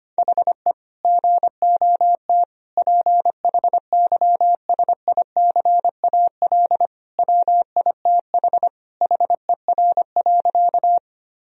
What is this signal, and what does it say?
Morse code, heard as a keyed tone, about 25 wpm, 700 Hz. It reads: HE GOT PHYSICAL WITH HER.
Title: Sentences from Top 1000 Words - Review Part 4 - 25wpm